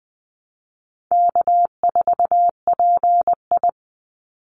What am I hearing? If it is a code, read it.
Morse code, heard as a keyed tone, about 20 wpm, 700 Hz. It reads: K4PI
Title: Easy Call Signs Repeated 3x - 20wpm